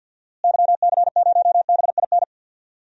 Morse code, heard as a keyed tone, about 47 wpm, 700 Hz. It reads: YC0BIN
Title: DXCC-100 Call Signs - 6 Characters - Repeated 3x - 50wpm